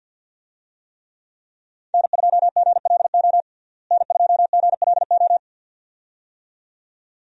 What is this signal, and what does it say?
Morse code, heard as a keyed tone, about 51 wpm, 700 Hz. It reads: N2GLO N2GLO
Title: USA Call Signs - 5 Characters - Repeated 2x - 50wpm